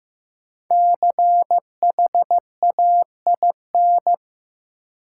Morse code, heard as a keyed tone, about 15 wpm, 700 Hz. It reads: CHAIN
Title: Sentences from Top 1500 Words - New Word Drill Alt - 15wpm